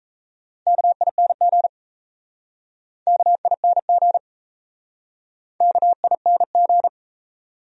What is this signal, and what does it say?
Morse code, heard as a keyed tone, about 37 wpm, 700 Hz. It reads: KING KING KING